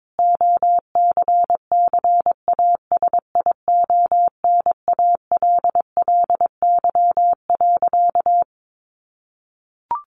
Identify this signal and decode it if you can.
Morse code, heard as a keyed tone, about 22 wpm, 700 Hz. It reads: OCCASIONALLY.
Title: Sentences from Top 1300 Words 22wpm